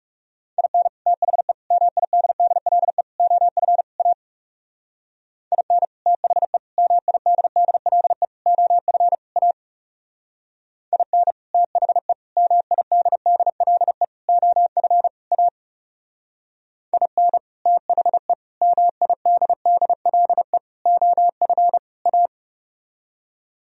Morse code, heard as a keyed tone, about 36 wpm, 700 Hz. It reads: IN THE MIDDLE OF A IN THE MIDDLE OF A IN THE MIDDLE OF A IN THE MIDDLE OF A
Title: Speed-Racing - Sets of 5 Words 30wpm